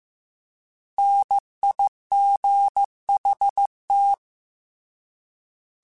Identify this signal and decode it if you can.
Morse code, heard as a keyed tone, about 15 wpm, 800 Hz. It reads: NIGHT
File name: Alphabetical / Alphabetical mp3s / Ns @13